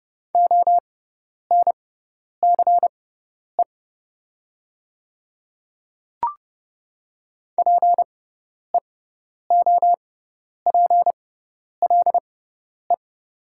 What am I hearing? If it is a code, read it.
Morse code, heard as a keyed tone, about 30 wpm, spaced as for 10 wpm, 700 Hz. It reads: ONCE PEOPLE
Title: 6x Character Spacing - Top 300 Words - 30wpm